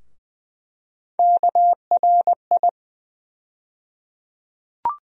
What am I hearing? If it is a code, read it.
Morse code, heard as a keyed tone, about 20 wpm, 700 Hz. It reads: KRI
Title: USA Call Sign Postfixes - 3 Letters 20wpm